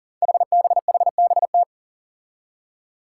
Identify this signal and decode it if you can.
Morse code, heard as a keyed tone, about 40 wpm, 700 Hz. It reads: HBHBT
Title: Five Challenging Letter-Number Combinations 40wpm